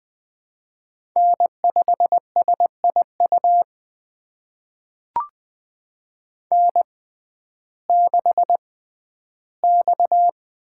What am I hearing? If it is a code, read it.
Morse code, heard as a keyed tone, about 20 wpm, spaced as for 12 wpm, 700 Hz. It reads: N5SIU N6X
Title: USA Call Signs - 5 Characters - 6x Character Spacing - 20wpm